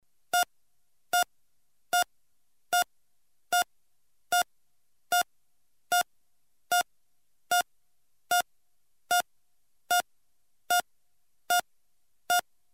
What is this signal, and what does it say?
There is no code.